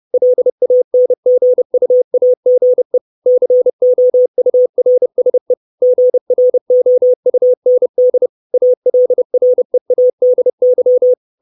Morse code, heard as a keyed tone, about 30 wpm, 500 Hz. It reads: LANGUAGE COURSE GROUND ALREADY
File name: 632 Words 30WPM